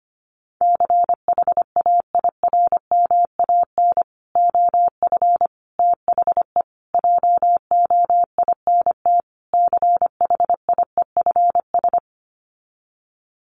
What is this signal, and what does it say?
Morse code, heard as a keyed tone, about 25 wpm, 700 Hz. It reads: CHAIRMAN OF THE JOINT CHIEFS
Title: Sets of 5 Words Encore 25wpm